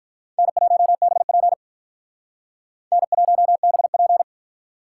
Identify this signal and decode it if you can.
Morse code, heard as a keyed tone, about 44 wpm, 700 Hz. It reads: N1BP N1BP